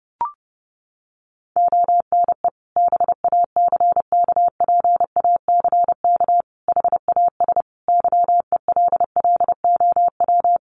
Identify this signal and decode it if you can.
Morse code, heard as a keyed tone, about 28 wpm, 700 Hz. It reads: ONE BACKPACK HAS YELLOW
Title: Additional Easy Phrases 30wpm